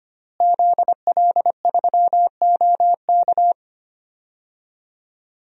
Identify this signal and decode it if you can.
Morse code, heard as a keyed tone, about 25 wpm, 700 Hz. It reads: ZL3OK
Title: DXCC-100 Call Signs - 5 Characters - Repeated 2x - 25wpm